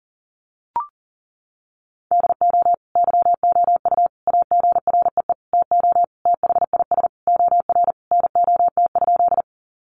Morse code, heard as a keyed tone, about 40 wpm, 700 Hz. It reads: DO YOU AGREE TO THIS OR NOT?
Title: Sentences from Top 1100 Words 40wpm